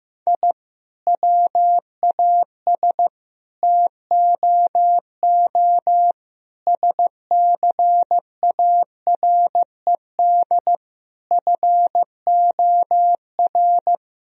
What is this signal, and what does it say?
Morse code, heard as a keyed tone, about 15 wpm, 700 Hz. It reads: I WAS TOO SCARED FOR
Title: Sentences from Top 1000 Words - Review Part 4 - 15wpm